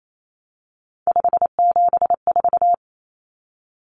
Morse code, heard as a keyed tone, about 28 wpm, 700 Hz. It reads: 574
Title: Numbers - 3-Digit 28wpm